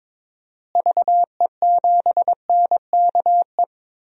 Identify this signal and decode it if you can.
Morse code, heard as a keyed tone, about 22 wpm, 700 Hz. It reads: VE7NKE
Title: DXCC-100 Call Signs - 6 Characters - Repeated 3x - 22wpm